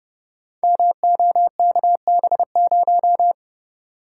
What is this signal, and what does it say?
Morse code, heard as a keyed tone, about 30 wpm, 700 Hz. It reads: MOKB0
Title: Code Groups - 5 Letter:Numbers - 30wpm